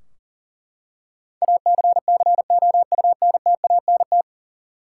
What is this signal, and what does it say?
Morse code, heard as a keyed tone, about 39 wpm, 700 Hz. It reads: ACCOUNTANT
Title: Practice QSO - Exchange 3 - More Info - 40wpm